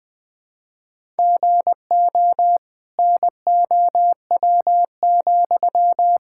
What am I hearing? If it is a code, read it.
Morse code, heard as a keyed tone, about 21 wpm, 700 Hz. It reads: GO NOW,
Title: Sentences from Top 1000 Words - Review Part 5 - 20wpm